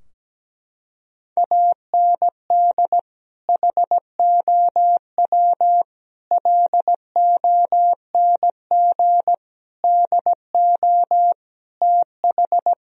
Morse code, heard as a keyed tone, about 17 wpm, 700 Hz. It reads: AND HOW LONG DO TH
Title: Sentences from Top 1000 Words - Review Part 4 - 17wpm